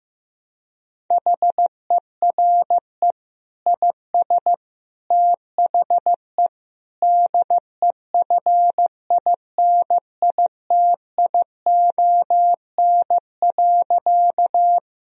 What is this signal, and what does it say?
Morse code, heard as a keyed tone, about 15 wpm, 700 Hz. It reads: HERE IS THE DEFINITION.
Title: Sentences from Top 1000 Words - Review Part 4 - 15wpm